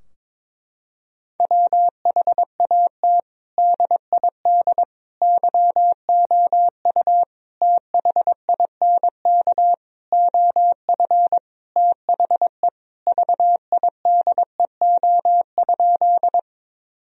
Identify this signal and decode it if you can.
Morse code, heard as a keyed tone, about 22 wpm, 700 Hz. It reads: WHAT DID YOU THINK OF THE VIDEO?
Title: Sentences from Top 1000 Words - Review Part 4 - 22wpm